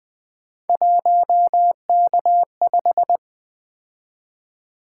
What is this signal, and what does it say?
Morse code, heard as a keyed tone, about 20 wpm, 700 Hz. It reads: 1K5
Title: Three Letter-Number Combinations - 20wpm